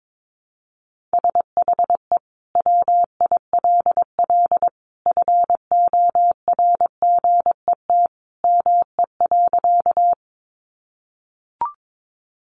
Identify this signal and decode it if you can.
Morse code, heard as a keyed tone, about 22 wpm, 700 Hz. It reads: SHE WILL FORGET ME.
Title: Sentences from Top 1200 Words 22wpm